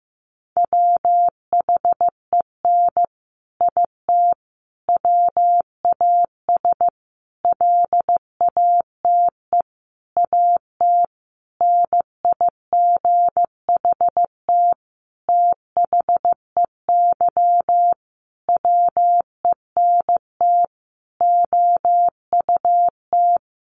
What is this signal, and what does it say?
Morse code, heard as a keyed tone, about 15 wpm, 700 Hz. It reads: WHEN IT WAS LATE AT NIGHT THEY WENT OUT
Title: Sentences from Top 600 Words 15wpm